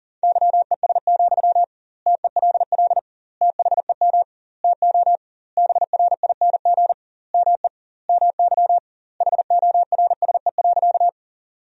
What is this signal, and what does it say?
Morse code, heard as a keyed tone, about 40 wpm, 700 Hz. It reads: YES, TELL THEM TO BRING ME MY HORSE.
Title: 2x Word-Spacing - Sentences from Top 500 Words - 40wpm